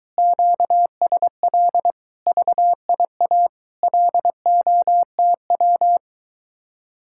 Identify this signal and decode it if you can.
Morse code, heard as a keyed tone, about 22 wpm, 700 Hz. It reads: QSL VIA LOTW